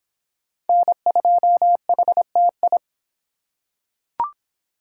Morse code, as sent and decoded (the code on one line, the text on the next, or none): -. ..--- .... - ..
N2HTI